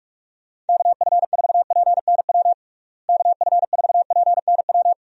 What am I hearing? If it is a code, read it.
Morse code, heard as a keyed tone, about 46 wpm, 700 Hz. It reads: KF4PNW KF4PNW
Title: USA Call Signs - 6 Characters - Repeated 2x - 45wpm